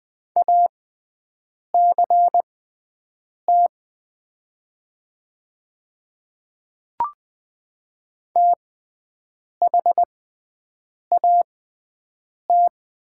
Morse code, heard as a keyed tone, about 20 wpm, spaced as for 7 wpm, 700 Hz. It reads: ACT THAT
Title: 6x Character Spacing - Top 300 Words - 20wpm